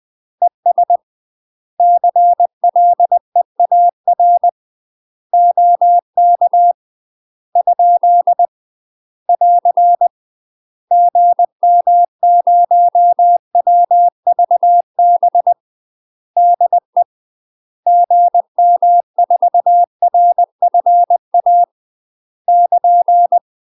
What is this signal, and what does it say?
Morse code, heard as a keyed tone, about 20 wpm, 700 Hz. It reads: ES CLEAR OK ? <AR> GM0WVB DE GM4RFA <KN>
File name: UK MASTER QSOs TRAINING / 20WPM / UK MASTER QSOs 0022